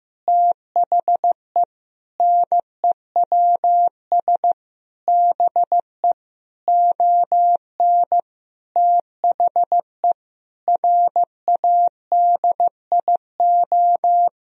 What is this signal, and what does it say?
Morse code, heard as a keyed tone, about 15 wpm, 700 Hz. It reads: THE NEWS BE ON THE RADIO
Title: Sentences from Top 1000 Words - Review Part 3 - 15wpm